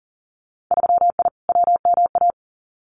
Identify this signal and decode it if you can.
Morse code, heard as a keyed tone, about 40 wpm, 700 Hz. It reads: 3I WMA
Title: Winter Field Day Exchanges 40wpm